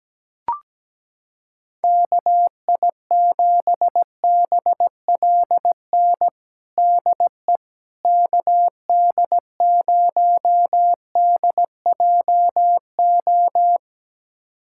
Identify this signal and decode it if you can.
Morse code, heard as a keyed tone, about 17 wpm, 700 Hz. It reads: KI7BLN DE KD0DJO
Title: USA Callsign DE Callsign 17wpm